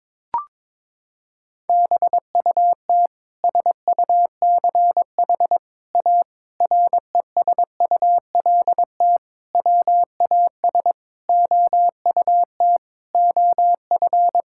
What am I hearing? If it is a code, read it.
Morse code, heard as a keyed tone, about 22 wpm, 700 Hz. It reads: BUT SUCH A RESULT WAS OUT OF
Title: Sentences from Top 1000 Words - Review Part 5 - 22wpm